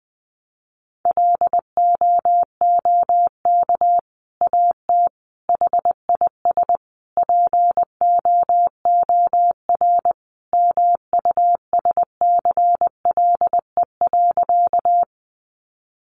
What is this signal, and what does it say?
Morse code, heard as a keyed tone, about 20 wpm, 700 Hz. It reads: LOOK AT HIS POOR MUSCLE.
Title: Sentences from Top 1300 Words 20wpm